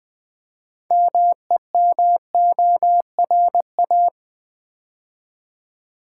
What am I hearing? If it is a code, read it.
Morse code, heard as a keyed tone, about 20 wpm, 700 Hz. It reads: MEMORA
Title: Build A Word - 3 Letters At A Time - 9-Letter Words - 20wpm